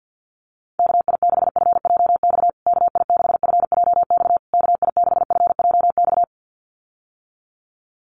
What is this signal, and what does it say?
Morse code, heard as a keyed tone, about 49 wpm, 700 Hz. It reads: KI6FJX KI6FJX KI6FJX
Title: USA Call Signs Repeated 3x 50wpm